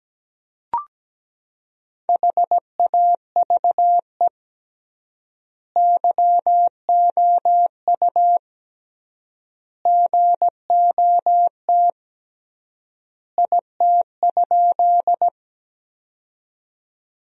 Morse code, heard as a keyed tone, about 17 wpm, 700 Hz. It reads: HAVE YOU GOT IT?
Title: 3x Word-Spacing - Sentences from Top 300 Words - 17wpm